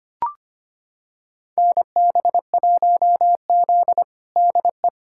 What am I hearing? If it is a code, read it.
Morse code, heard as a keyed tone, about 25 wpm, 700 Hz. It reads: NB1Z DE
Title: USA Callsign DE Callsign - 4 Characters - 25wpm